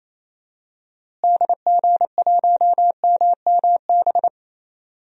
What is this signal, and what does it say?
Morse code, heard as a keyed tone, about 28 wpm, 700 Hz. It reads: DG1MMB
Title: DXCC-100 Call Signs - 6 Characters - Repeated 2x - 28wpm